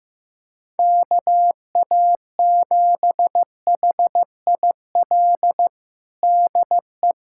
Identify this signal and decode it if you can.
Morse code, heard as a keyed tone, about 14 wpm, 700 Hz. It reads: KA7HIL DE
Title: USA Callsign DE Callsign - 6 Characters - 15wpm